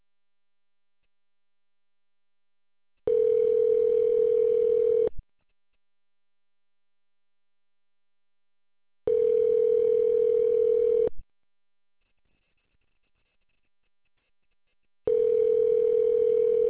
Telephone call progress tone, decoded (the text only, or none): ringback tone